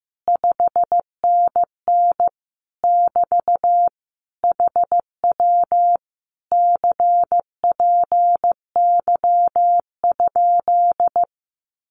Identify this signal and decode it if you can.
Morse code, heard as a keyed tone, about 14 wpm, 700 Hz. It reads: 5NN <BT> HW CPY?
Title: Common QSO Phrases Encore 15wpm